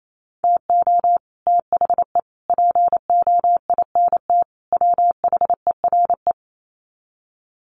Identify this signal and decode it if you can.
Morse code, heard as a keyed tone, about 28 wpm, 700 Hz. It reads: TO THE POINT WHERE
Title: Sets of 4 Words 28wpm